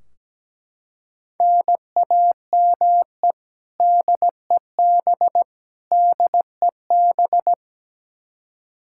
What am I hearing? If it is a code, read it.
Morse code, heard as a keyed tone, about 17 wpm, 700 Hz. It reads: NAME DEB DEB
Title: Practice QSO - Exchange 1 - Beginning - 17wpm